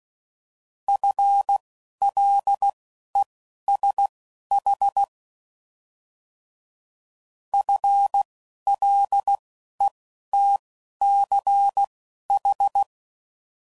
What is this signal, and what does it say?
Morse code, heard as a keyed tone, about 16 wpm, spaced as for 11 wpm, 800 Hz. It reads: FLESH FLETCH
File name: Prefixes / By First Letter / Fs